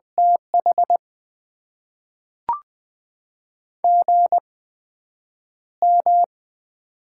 Morse code, heard as a keyed tone, about 20 wpm, spaced as for 9 wpm, 700 Hz. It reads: TH GM
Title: Top 20 QSO Elements - 8x Character Spacing - 20wpm